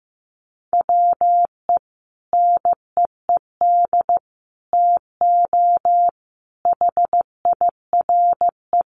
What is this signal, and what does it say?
Morse code, heard as a keyed tone, about 15 wpm, 700 Hz. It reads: WE NEED TO HIRE
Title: Sentences from Top 1400 Words 15wpm